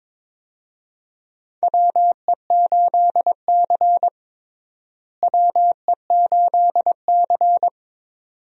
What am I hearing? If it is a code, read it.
Morse code, heard as a keyed tone, about 21 wpm, 700 Hz. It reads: WE8C WE8C
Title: USA Call Signs - 4 Characters - Repeated 2x - 22wpm